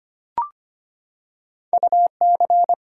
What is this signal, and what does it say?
Morse code, heard as a keyed tone, about 25 wpm, 700 Hz. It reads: UC
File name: Two Letter-Number Combinations - 25wpm